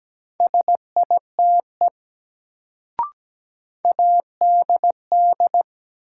Morse code, heard as a keyed tone, about 17 wpm, 700 Hz. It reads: SITE ADD
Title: Sentences from Top 1000 Words - Review Part 1 - New Word Drill - 17wpm